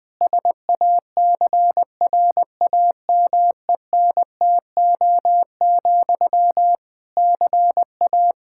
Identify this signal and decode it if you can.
Morse code, heard as a keyed tone, about 21 wpm, 700 Hz. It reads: SACRAMENTO, CA
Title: Practice QSO - Exchange 1 - Beginning - 20wpm